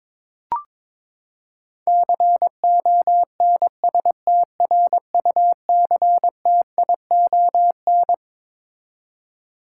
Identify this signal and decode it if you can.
Morse code, heard as a keyed tone, about 22 wpm, 700 Hz. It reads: CONSTRUCTION
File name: Twelve Letter Words - 22wpm